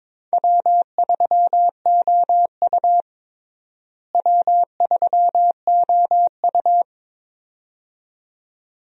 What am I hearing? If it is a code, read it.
Morse code, heard as a keyed tone, about 22 wpm, 700 Hz. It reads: W3OU W3OU
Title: USA Call Signs - 4 Characters - Repeated 2x - 22wpm